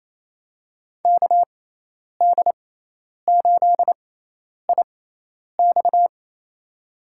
Morse code, heard as a keyed tone, about 28 wpm, spaced as for 10 wpm, 700 Hz. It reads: KD8IX
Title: USA Call Signs - 6x Character Spacing - 28wpm